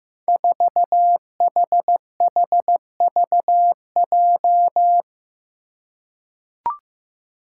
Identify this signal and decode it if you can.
Morse code, heard as a keyed tone, about 15 wpm, 700 Hz. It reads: 4HHVJ
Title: Five Challenging Letter-Number Combinations 15wpm